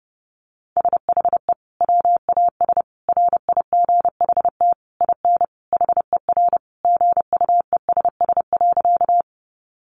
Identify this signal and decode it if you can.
Morse code, heard as a keyed tone, about 30 wpm, 700 Hz. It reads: SHE WAS RIGHT IN HER GUESS.
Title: Sentences from Top 1500 Words 30wpm